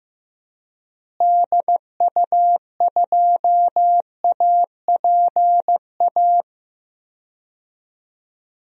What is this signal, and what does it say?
Morse code, heard as a keyed tone, about 15 wpm, 700 Hz. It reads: DU2APA
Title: DXCC-100 Call Signs - 6 Characters - Repeated 3x - 15wpm